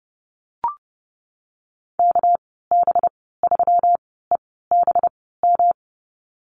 Morse code, heard as a keyed tone, about 30 wpm, spaced as for 17 wpm, 700 Hz. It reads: KB3EBM
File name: Lesson 37g - Letter x - Call Signs 30wpm